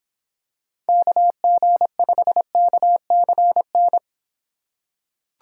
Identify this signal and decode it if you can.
Morse code, heard as a keyed tone, about 25 wpm, 700 Hz. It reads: KG5KCN